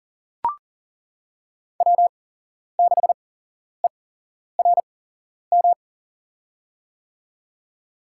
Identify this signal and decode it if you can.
Morse code, heard as a keyed tone, about 40 wpm, spaced as for 11 wpm, 700 Hz. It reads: W6ERM
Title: USA Call Signs - 5 Characters - 8x Character Spacing - 40wpm